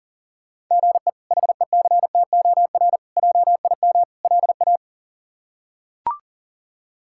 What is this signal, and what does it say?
Morse code, heard as a keyed tone, about 37 wpm, 700 Hz. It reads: GE HECTOR JIM LA
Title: K1USN SST QSO Practice - 40wpm